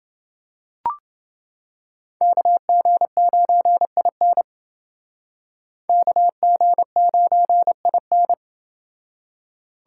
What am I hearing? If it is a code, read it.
Morse code, heard as a keyed tone, about 29 wpm, 700 Hz. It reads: KG9IN KG9IN